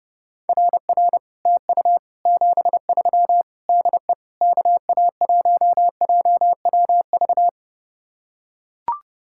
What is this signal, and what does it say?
Morse code, heard as a keyed tone, about 30 wpm, 700 Hz. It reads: RR TU 73 DE KA1JWV